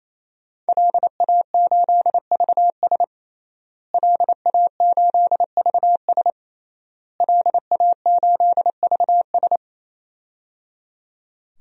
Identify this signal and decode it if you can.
Morse code, heard as a keyed tone, about 28 wpm, 700 Hz. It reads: LA8VS LA8VS LA8VS
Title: DXCC-100 Call Signs - 5 Characters - Repeated 3x - 28wpm